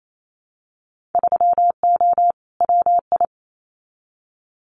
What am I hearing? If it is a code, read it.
Morse code, heard as a keyed tone, about 28 wpm, 700 Hz. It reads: 3O WI
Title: Winter Field Day Exchanges 28wpm